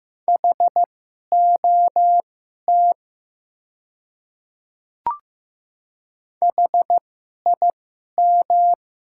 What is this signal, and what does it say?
Morse code, heard as a keyed tone, about 15 wpm, spaced as for 11 wpm, 700 Hz. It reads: HOT HIM
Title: 2x Character Spacing - Top 500 Words - 15wpm